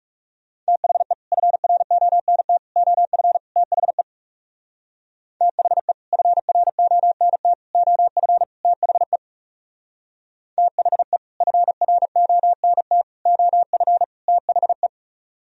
Morse code, heard as a keyed tone, about 36 wpm, 700 Hz. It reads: THE FRONT OF THE THE FRONT OF THE THE FRONT OF THE
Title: Speed-Racing - Sets of 4 Words 30wpm